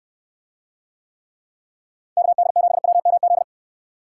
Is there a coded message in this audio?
Yes: Morse code from 2.2 to 3.4 s, heard as a keyed tone, about 69 wpm, 700 Hz. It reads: KD7WMZ